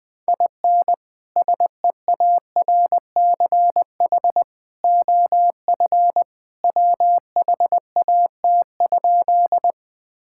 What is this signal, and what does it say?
Morse code, heard as a keyed tone, about 20 wpm, 700 Hz. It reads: IN SEARCH OF WHAT?
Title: Sentences from Top 1000 Words - Review Part 5 - 20wpm